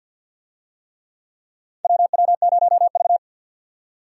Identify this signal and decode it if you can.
Morse code, heard as a keyed tone, about 48 wpm, 700 Hz. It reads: WW0V
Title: USA Call Signs - 4 Characters - 50wpm